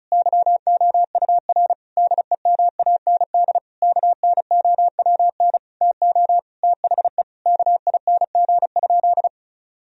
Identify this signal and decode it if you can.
Morse code, heard as a keyed tone, about 36 wpm, 700 Hz. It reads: YOUR DEMAND KNOWN TO THE KING?
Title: Sentences from Top 1000 Words - Review Part 4 - 35wpm